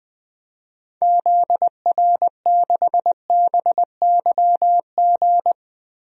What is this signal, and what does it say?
Morse code, heard as a keyed tone, about 20 wpm, 700 Hz. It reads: ZR6BYG
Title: DXCC-100 Call Signs - 6 Characters - Repeated 2x - 20wpm